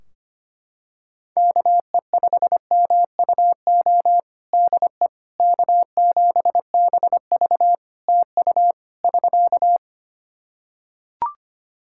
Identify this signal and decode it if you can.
Morse code, heard as a keyed tone, about 25 wpm, 700 Hz. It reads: KE5MUO DE K7BV TU <SK>
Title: K1USN SST QSO Practice 25wpm